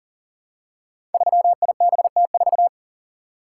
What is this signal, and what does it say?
Morse code, heard as a keyed tone, about 39 wpm, 700 Hz. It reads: 3IBT4
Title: Code Groups - 5 Letter:Numbers - 40wpm